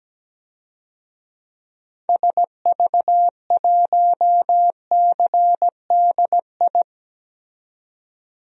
Morse code, heard as a keyed tone, about 17 wpm, 700 Hz. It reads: SV1CDI
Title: DXCC-100 Call Signs - 6 Characters - 17wpm